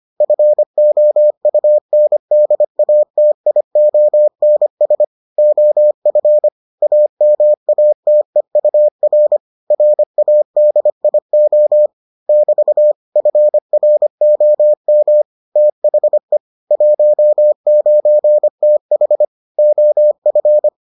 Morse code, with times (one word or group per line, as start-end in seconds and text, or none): 0.2-5.0 FOUNDATIONS
5.4-6.5 OF
6.8-9.4 AMATEUR
9.7-11.9 RADIO
12.3-12.9 <BT>
13.2-15.2 FROM
15.6-16.4 THE
16.7-19.2 19TH
19.6-20.7 OF